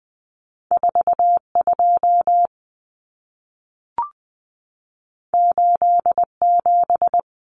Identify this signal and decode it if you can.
Morse code, heard as a keyed tone, about 20 wpm, 700 Hz. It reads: 42 87
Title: Common Cut Numbers - 2-Digit 20wpm